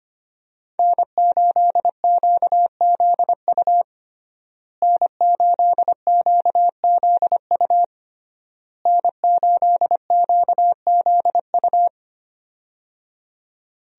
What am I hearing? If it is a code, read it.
Morse code, heard as a keyed tone, about 25 wpm, 700 Hz. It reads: N8QZU N8QZU N8QZU
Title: USA Call Signs - 5 Characters - Repeated 3x - 25wpm